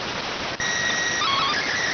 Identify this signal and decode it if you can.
SSTV (slow-scan television), PD 290.